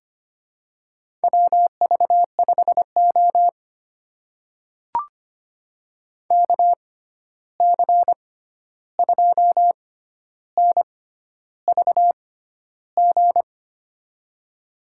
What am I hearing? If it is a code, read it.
Morse code, heard as a keyed tone, about 25 wpm, spaced as for 11 wpm, 700 Hz. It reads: WV5O KC2NVG
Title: USA Call Signs - 6x Character Spacing - 25wpm